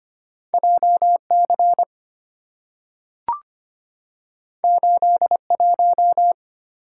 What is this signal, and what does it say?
Morse code, heard as a keyed tone, about 25 wpm, 690 Hz. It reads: JC 81
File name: Two Letter-Number Combinations - 25wpm